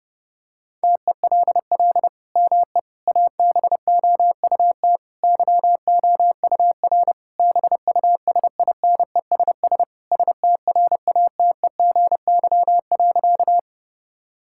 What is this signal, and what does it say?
Morse code, heard as a keyed tone, about 32 wpm, 700 Hz. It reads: TELL ME ABOUT YOUR BUSINESS STRATEGY.
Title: Sentences from Top 1000 Words - Review Part 5 - 30wpm